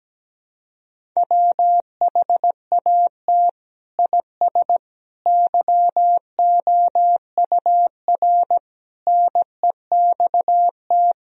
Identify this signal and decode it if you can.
Morse code, heard as a keyed tone, about 17 wpm, 700 Hz. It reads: WHAT IS YOUR NEXT
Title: Sentences from Top 1000 Words - Review Part 1 - 17wpm